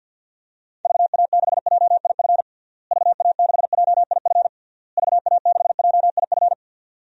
Morse code, heard as a keyed tone, about 48 wpm, 700 Hz. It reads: VA6JIF VA6JIF VA6JIF
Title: DXCC-100 Call Signs - 6 Characters - Repeated 3x - 50wpm